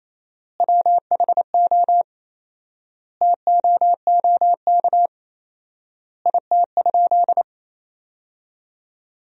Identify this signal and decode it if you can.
Morse code, heard as a keyed tone, about 27 wpm, 700 Hz. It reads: WHO TOOK IT?
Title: 4x Word-Spacing - Sentences from Top 300 Words - 28wpm